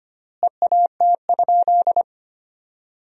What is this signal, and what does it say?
Morse code, heard as a keyed tone, about 25 wpm, 700 Hz. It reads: EAT?
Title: 4x Word-Spacing - Sentences from Top 500 Words - 25wpm